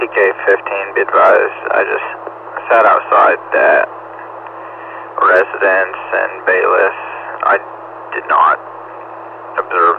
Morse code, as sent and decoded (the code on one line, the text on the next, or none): none